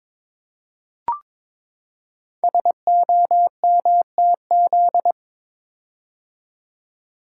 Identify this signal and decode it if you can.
Morse code, heard as a keyed tone, about 22 wpm, 700 Hz. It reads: SOMTZ
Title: Code Groups - 5 Letter:Numbers - 22wpm